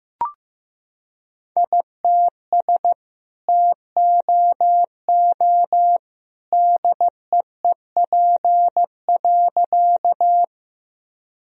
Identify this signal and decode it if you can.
Morse code, heard as a keyed tone, about 15 wpm, 700 Hz. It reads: ITS TOO DEEP.
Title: Sentences from Top 1000 Words - Review Part 3 - 15wpm